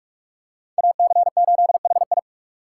Morse code, heard as a keyed tone, about 42 wpm, 690 Hz. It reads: AC8HI
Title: USA Call Signs - 6x Character Spacing - 45wpm